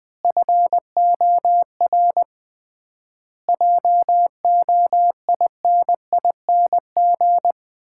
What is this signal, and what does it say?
Morse code, heard as a keyed tone, about 20 wpm, 700 Hz. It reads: FOR JOINING
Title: 3x Word-Spacing - Sets of 5 Words - 20wpm